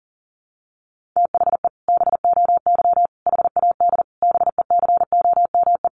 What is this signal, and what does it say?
Morse code, heard as a keyed tone, about 37 wpm, 700 Hz. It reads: THE BOY HAD BECOME
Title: Sentences from Top 1500 Words 40wpm